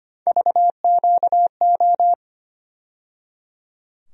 Morse code, heard as a keyed tone, about 25 wpm, 700 Hz. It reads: VQO
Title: USA Call Sign Postfixes - 3 Letters 25wpm